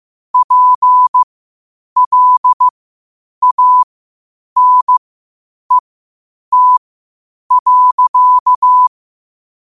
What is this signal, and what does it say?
Morse code, heard as a keyed tone, about 15 wpm, spaced as for 8 wpm, 1000 Hz. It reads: PLANET.
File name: qotdfiles / 8wpm